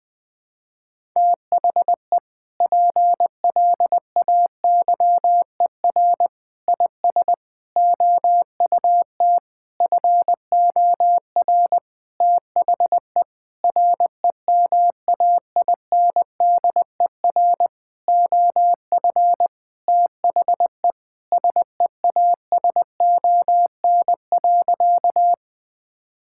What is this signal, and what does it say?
Morse code, heard as a keyed tone, about 20 wpm, 700 Hz. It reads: THE PLAYER IS OUT FOR THE REMAINDER OF THE SEASON.
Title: Sentences with 6-Word Phrases - Spoken Beginning - 20wpm